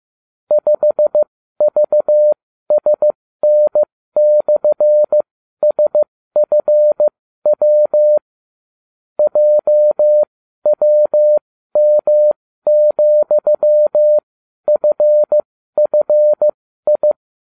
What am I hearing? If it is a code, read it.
Morse code, heard as a keyed tone, about 15 wpm, spaced as for 12 wpm, 600 Hz. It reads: 5VSN/SFW JWM,FFI